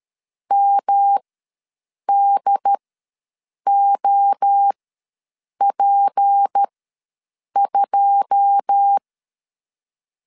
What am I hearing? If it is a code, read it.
Morse code, heard as a keyed tone, about 13 wpm, spaced as for 7 wpm, 800 Hz. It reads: MDOP2
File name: LEZIONI / 10 SIGNS / 10A SLASH ws 12wpm 3443